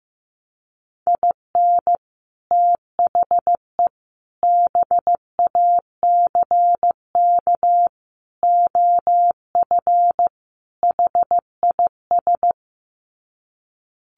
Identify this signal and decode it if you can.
Morse code, heard as a keyed tone, about 15 wpm, 700 Hz. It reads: IN THE BACK OF HIS
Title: Sets of 5 Words Encore2 15wpm